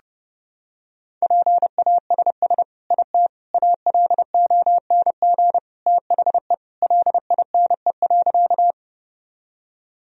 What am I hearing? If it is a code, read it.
Morse code, heard as a keyed tone, about 30 wpm, 700 Hz. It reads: PASS IT ALONG THE LINE.
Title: Sentences from Top 1000 Words - Review Part 1 - 30wpm